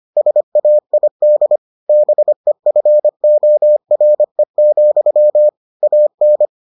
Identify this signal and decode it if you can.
Morse code, heard as a keyed tone, about 25 wpm, 600 Hz. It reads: SAID BEFORE, AN